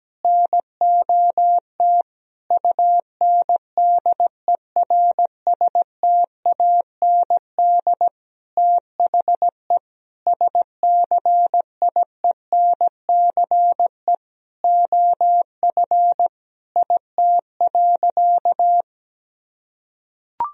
Morse code, heard as a keyed tone, about 17 wpm, 700 Hz. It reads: NOT UNDERSTAND THE SCIENCE OF IT.
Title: Sentences from Top 1000 Words - Review Part 2 - 17wpm